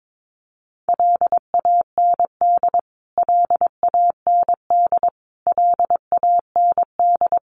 Morse code, heard as a keyed tone, about 22 wpm, 700 Hz. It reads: LAND LAND LAND
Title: Top 500 Words Repeated 22wpm